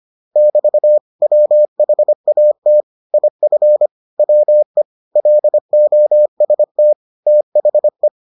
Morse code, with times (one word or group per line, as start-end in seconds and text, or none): 0.4-1.0 <BT>
1.2-2.8 WHAT
3.1-3.9 IF
4.2-4.8 WE
5.2-6.9 LOST
7.3-8.1 THE